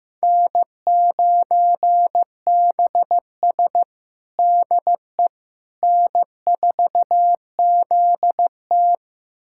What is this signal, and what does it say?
Morse code, heard as a keyed tone, about 15 wpm, 700 Hz. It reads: N9BS DE N4ZT
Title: USA Callsign DE Callsign - 4 Characters - 15wpm